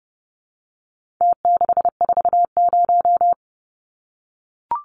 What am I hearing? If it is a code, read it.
Morse code, heard as a keyed tone, about 30 wpm, 700 Hz. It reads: T640
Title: Numbers - Contest Style - 4-Digit 30wpm